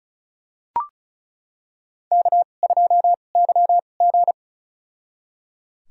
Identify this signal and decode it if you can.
Morse code, heard as a keyed tone, about 36 wpm, spaced as for 25 wpm, 700 Hz. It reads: K2YG
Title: USA Call Signs - 2x Character Spacing - 35wpm